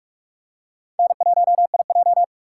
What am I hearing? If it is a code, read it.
Morse code, heard as a keyed tone, about 43 wpm, 690 Hz. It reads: N1IJ
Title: USA Call Signs - 4x Character Spacing - 45wpm